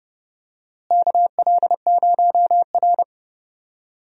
Morse code, heard as a keyed tone, about 30 wpm, 700 Hz. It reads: KL0R
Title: USA Call Signs - 4 Characters - Repeated 2x - 30wpm